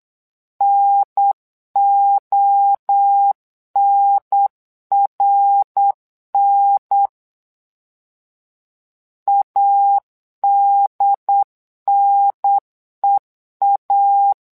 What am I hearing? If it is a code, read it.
Morse code, heard as a keyed tone, about 8 wpm, 800 Hz. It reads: NONRN ADNEA
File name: LEZIONI / 3 MWRO / 3 MWRO 8wpm 2606